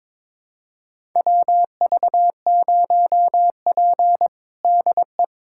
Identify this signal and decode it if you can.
Morse code, heard as a keyed tone, about 22 wpm, 700 Hz. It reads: WV0P DE
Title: USA Callsign DE Callsign - 4 Characters - 22wpm